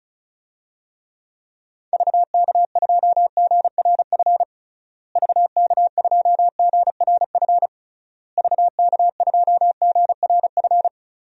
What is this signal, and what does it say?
Morse code, heard as a keyed tone, about 36 wpm, 700 Hz. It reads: VK2GRF VK2GRF VK2GRF
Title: DXCC-100 Call Signs - 6 Characters - Repeated 3x - 35wpm